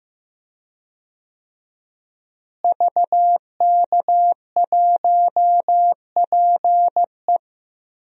Morse code, heard as a keyed tone, about 15 wpm, 700 Hz. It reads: VK1PE
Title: DXCC-100 Call Signs - 5 Characters - Repeated 3x - 15wpm